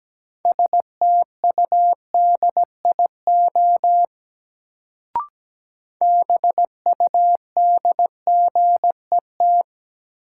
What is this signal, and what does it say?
Morse code, heard as a keyed tone, about 17 wpm, 700 Hz. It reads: STUDIO BUDGET
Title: Sentences from Top 1200 Words - New Word Drill Alt - 17wpm